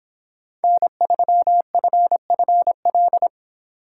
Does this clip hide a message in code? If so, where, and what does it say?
Yes: Morse code from 0.6 to 3.3 s, heard as a keyed tone, about 30 wpm, 700 Hz. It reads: N3FFL